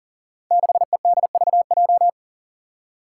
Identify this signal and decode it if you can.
Morse code, heard as a keyed tone, about 39 wpm, 700 Hz. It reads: 6EDVJ
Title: Code Groups - 5 Letter:Numbers - 40wpm